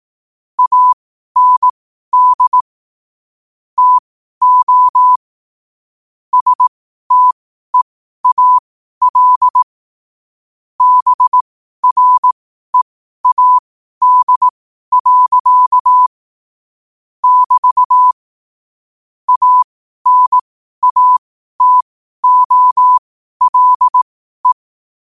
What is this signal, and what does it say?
Morse code, heard as a keyed tone, about 18 wpm, spaced as for 12 wpm, 1000 Hz. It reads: AND TO STEAL BREAD. <BT> ANATOLE
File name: qotdfiles / 12wpm